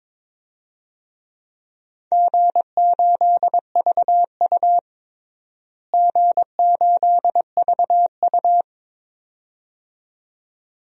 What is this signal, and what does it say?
Morse code, heard as a keyed tone, about 22 wpm, 700 Hz. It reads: G8VU G8VU
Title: DXCC-100 Call Signs - 4 Characters - Repeated 2x - 22wpm